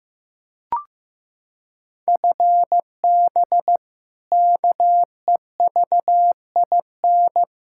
Morse code, heard as a keyed tone, about 16 wpm, 700 Hz. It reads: FB KEVIN
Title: Practice QSO - Exchange 3 - More Info - 15wpm